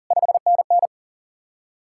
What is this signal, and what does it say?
Morse code, heard as a keyed tone, about 48 wpm, 700 Hz. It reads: 5NN
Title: Speed-Racing Common QSO Phrases Encore - 40wpm